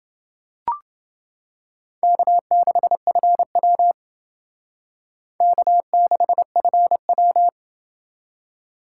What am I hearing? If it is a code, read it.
Morse code, heard as a keyed tone, about 28 wpm, 700 Hz. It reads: K6FW K6FW